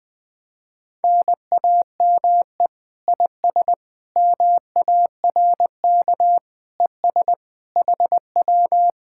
Morse code, heard as a keyed tone, about 20 wpm, 700 Hz. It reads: NAME IS MARK ES HW
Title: Speed-Racing Common QSO Phrases - 20wpm